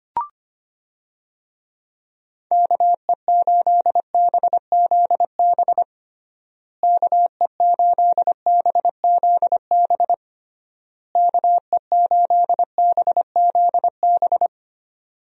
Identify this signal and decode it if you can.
Morse code, heard as a keyed tone, about 25 wpm, 700 Hz. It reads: KE8BZB KE8BZB KE8BZB
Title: USA Call Signs - 6 Characters - Repeated 3x - 25wpm